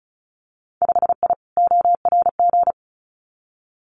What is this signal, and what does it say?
Morse code, heard as a keyed tone, about 36 wpm, 700 Hz. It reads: 5I ORG
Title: Winter Field Day Exchanges Encore 35wpm